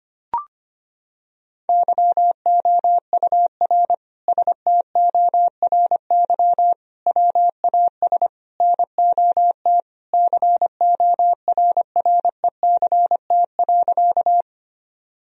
Morse code, heard as a keyed tone, about 25 wpm, 700 Hz. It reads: YOUR STORY WAS NOT CORRECT.
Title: Sentences from Top 1000 Words - Review Part 2 - 25wpm